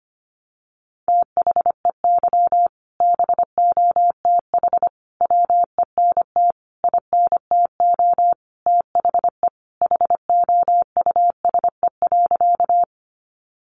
Morse code, heard as a keyed tone, about 25 wpm, 700 Hz. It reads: THEY BOTH WENT INTO THE HOUSE.
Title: Sentences from Top 600 Words 25wpm